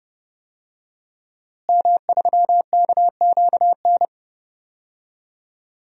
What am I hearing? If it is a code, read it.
Morse code, heard as a keyed tone, about 32 wpm, 700 Hz. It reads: M3KQN
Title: DXCC-100 Call Signs - 5 Characters - Repeated 3x - 30wpm